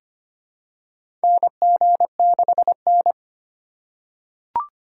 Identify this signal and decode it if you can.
Morse code, heard as a keyed tone, about 25 wpm, 700 Hz. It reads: NG6N
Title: USA Call Signs - 4 Characters - 2x Character Spacing - 25wpm